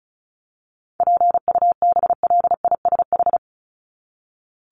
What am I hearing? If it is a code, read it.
Morse code, heard as a keyed tone, about 34 wpm, 700 Hz. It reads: PUBLISH
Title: ED Suffix - Top 100 Words with Base Words 35wpm